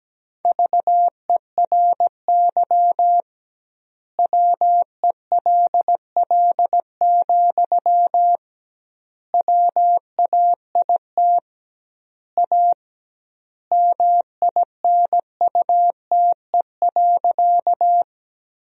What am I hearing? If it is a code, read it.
Morse code, heard as a keyed tone, about 17 wpm, 700 Hz. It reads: VERY WELL, WAIT A MINUTE.
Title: 2x Word-Spacing - Sentences from Top 500 Words - 17wpm